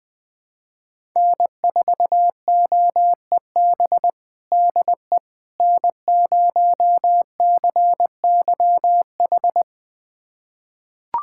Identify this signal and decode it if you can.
Morse code, heard as a keyed tone, about 21 wpm, 700 Hz. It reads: N4OEB DE N0CYH
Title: USA Callsign DE Callsign 20wpm